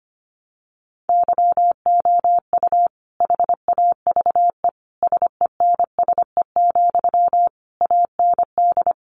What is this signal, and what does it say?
Morse code, heard as a keyed tone, about 25 wpm, 700 Hz. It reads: YOU HAVE SENSE, AND
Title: Sentences from Top 1100 Words 25wpm